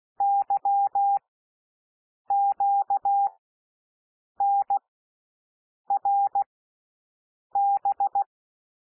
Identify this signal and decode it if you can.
Morse code, heard as a keyed tone, about 16 wpm, spaced as for 6 wpm, 800 Hz. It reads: YQNRB
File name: cwmp3 / L1 SOUND LETTERS 16 wpm